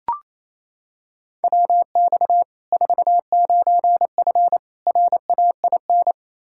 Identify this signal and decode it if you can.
Morse code, heard as a keyed tone, about 28 wpm, 700 Hz. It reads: WX 49F RAIN
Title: Practice QSO - Exchange 2 - Info - 28wpm